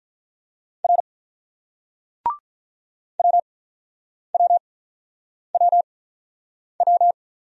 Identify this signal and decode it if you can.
Morse code, heard as a keyed tone, about 42 wpm, spaced as for 9 wpm, 700 Hz. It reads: R WWWW
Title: Speed-Racing Single Letters 35wpm